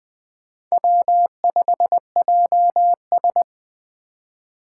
Morse code, heard as a keyed tone, about 21 wpm, 700 Hz. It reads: W5JS